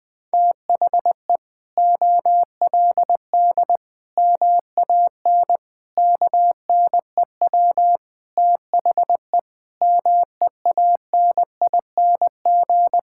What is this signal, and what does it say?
Morse code, heard as a keyed tone, about 21 wpm, 700 Hz. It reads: THE OLD MAN KNEW THE MEANING
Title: Sentences from Top 1000 Words - Review Part 6 - 20wpm